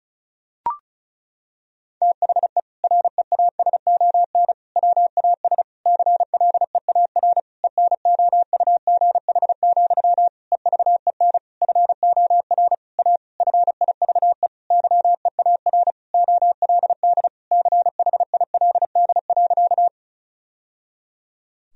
Morse code, heard as a keyed tone, about 34 wpm, 700 Hz. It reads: THE REASON WAS CLEAR ENOUGH, EVEN FOR A FIVE YEAR OLD CHILD.
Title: Sentences from Top 1000 Words - Review Part 2 - 35wpm